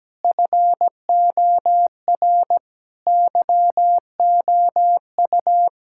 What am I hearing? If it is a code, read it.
Morse code, heard as a keyed tone, about 17 wpm, 700 Hz. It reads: FOR YOU
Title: Sentences from Top 1000 Words - Review Part 6 - 17wpm